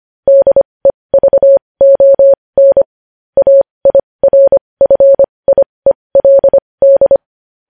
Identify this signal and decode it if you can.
Morse code, heard as a keyed tone, about 25 wpm, spaced as for 20 wpm, 550 Hz. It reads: DEVON AIRFIELD
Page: News headlines in Morse code at 20 WPM